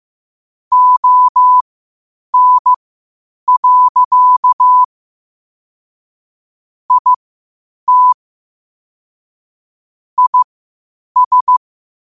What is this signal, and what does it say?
Morse code, heard as a keyed tone, about 15 wpm, spaced as for 8 wpm, 1000 Hz. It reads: ON. IT IS
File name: qotdfiles / 8wpm